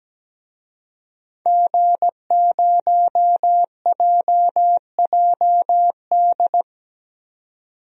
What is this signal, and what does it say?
Morse code, heard as a keyed tone, about 17 wpm, 700 Hz. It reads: G0JJD
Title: DXCC-100 Call Signs - 5 Characters - 17wpm